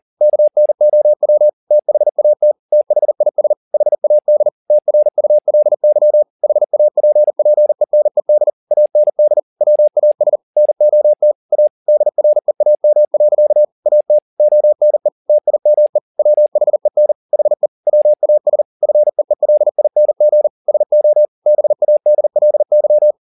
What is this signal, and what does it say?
Morse code, heard as a keyed tone, about 40 wpm, 600 Hz. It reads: KNOW THAT THIS HAD TRULY HAPPENED AND WAS NOT A DREAM. AT ONE TIME WHEN HE WAS FEELING SO BADLY
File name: the old man and the sea 2 40wpm